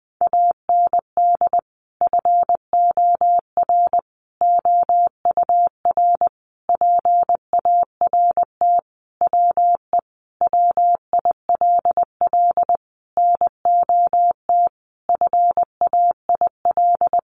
Morse code, heard as a keyed tone, about 20 wpm, 700 Hz. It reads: AND FOR OUR PART WE WILL NOT FAIL
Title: Sentences from Top 1100 Words 20wpm